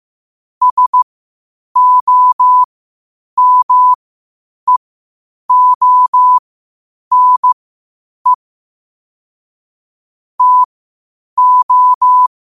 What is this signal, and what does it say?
Morse code, heard as a keyed tone, about 15 wpm, spaced as for 8 wpm, 1000 Hz. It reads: SOMEONE TO
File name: qotdfiles / 8wpm